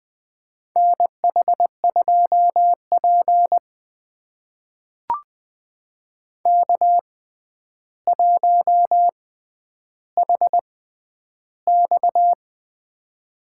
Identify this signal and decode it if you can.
Morse code, heard as a keyed tone, about 20 wpm, spaced as for 10 wpm, 700 Hz. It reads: NH2P K1HX
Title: USA Call Signs - 4 Characters - 6x Character Spacing - 20wpm